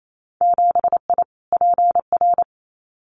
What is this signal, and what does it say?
Morse code, heard as a keyed tone, about 28 wpm, 700 Hz. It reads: 7I PR
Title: Winter Field Day Exchanges Encore 28wpm